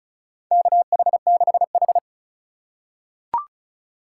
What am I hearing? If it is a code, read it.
Morse code, heard as a keyed tone, about 36 wpm, 700 Hz. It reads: KH6H
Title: USA Call Signs - 4 Characters - 2x Character Spacing - 35wpm